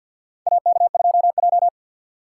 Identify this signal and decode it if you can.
Morse code, heard as a keyed tone, about 51 wpm, 700 Hz. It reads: AK2J
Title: USA Call Signs - 4 Characters - 3x Character Spacing - 50wpm